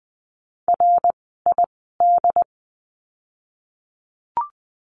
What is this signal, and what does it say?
Morse code, heard as a keyed tone, about 20 wpm, spaced as for 14 wpm, 700 Hz. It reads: RID
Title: 068 Lesson 12f - Letter d - Words 20wpm